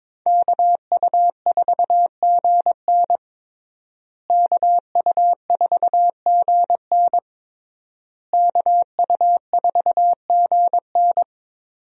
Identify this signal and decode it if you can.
Morse code, heard as a keyed tone, about 21 wpm, 700 Hz. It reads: KU4GN KU4GN KU4GN
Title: USA Call Signs - 5 Characters - Repeated 3x - 22wpm